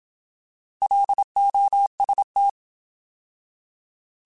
Morse code, heard as a keyed tone, about 27 wpm, 800 Hz. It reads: LOST
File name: Alphabetical / Alphabetical mp3s / Ls @25